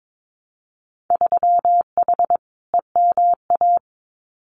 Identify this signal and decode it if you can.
Morse code, heard as a keyed tone, about 22 wpm, 700 Hz. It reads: 3H EMA
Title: Winter Field Day Exchanges Encore 22wpm